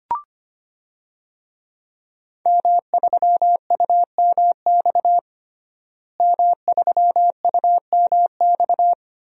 Morse code, heard as a keyed tone, about 25 wpm, 700 Hz. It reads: M3UMX M3UMX
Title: DXCC-100 Call Signs - 5 Characters - Repeated 3x - 25wpm